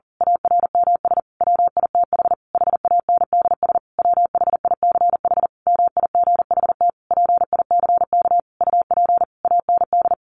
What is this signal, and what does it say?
Morse code, heard as a keyed tone, about 40 wpm, 700 Hz. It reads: ARMS WITH HANDS WHICH MIGHT PICK UP AND
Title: A Sound Of Thunder 40wpm